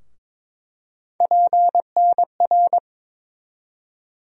Morse code, heard as a keyed tone, about 22 wpm, 700 Hz. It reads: PNR
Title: USA Call Sign Postfixes - 3 Letters 22wpm